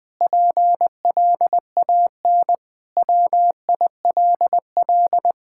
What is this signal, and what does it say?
Morse code, heard as a keyed tone, about 20 wpm, 700 Hz. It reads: PLAN WILL
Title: Sentences with 5-Word Phrases - Spoken Beginning - 20wpm